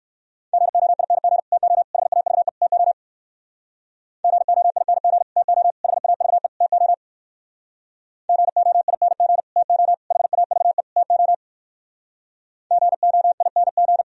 Morse code, heard as a keyed tone, about 67 wpm, 700 Hz. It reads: GOING TO HAVE TO GOING TO HAVE TO GOING TO HAVE TO GOING